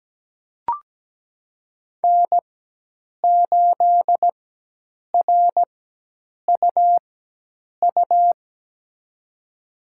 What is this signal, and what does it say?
Morse code, heard as a keyed tone, about 17 wpm, spaced as for 8 wpm, 700 Hz. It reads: N8RUU
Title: USA Call Signs - 5 Characters - 4x Character Spacing - 17wpm